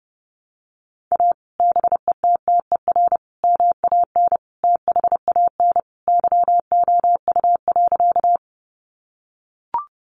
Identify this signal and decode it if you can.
Morse code, heard as a keyed tone, about 30 wpm, 700 Hz. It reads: A BETTER MAN THAN YOU.
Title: Sentences from Top 400 Words 30wpm